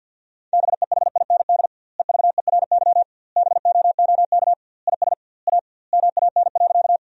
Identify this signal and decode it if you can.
Morse code, heard as a keyed tone, about 50 wpm, 700 Hz. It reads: BEHIND EVERY BOOK IS A MAN.
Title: 2x Word-Spacing - Sentences from Top 500 Words - 50wpm